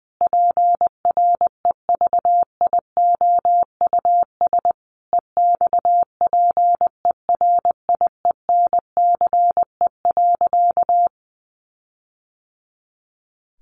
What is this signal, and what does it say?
Morse code, heard as a keyed tone, about 20 wpm, 700 Hz. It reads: PREVIOUS EXPERIENCE.
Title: Sentences from Top 1500 Words 20wpm